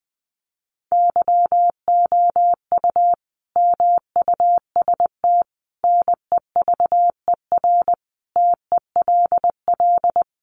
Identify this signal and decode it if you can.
Morse code, heard as a keyed tone, about 20 wpm, 700 Hz. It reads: YOU MUST NEVER TELL
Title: Sentences from Top 900 Words 20wpm